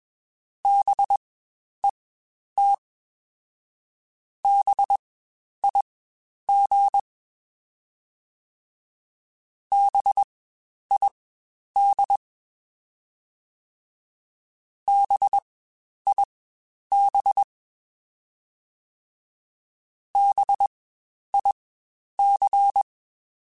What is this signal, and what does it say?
Morse code, heard as a keyed tone, about 21 wpm, spaced as for 10 wpm, 800 Hz. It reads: BET BIG BID BIB BIC
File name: Spaced Alphas / Spaced Alphas mp3s / Bs sp @20